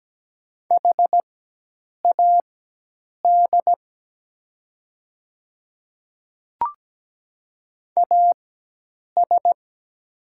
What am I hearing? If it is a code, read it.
Morse code, heard as a keyed tone, about 17 wpm, spaced as for 8 wpm, 700 Hz. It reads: HAD AS
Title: 4x Character Spacing - Top 200 Words - 17wpm